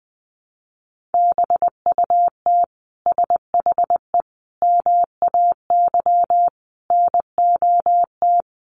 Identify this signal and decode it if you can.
Morse code, heard as a keyed tone, about 20 wpm, 700 Hz. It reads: BUT SHE MAY NOT
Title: Sentences from Top 600 Words 20wpm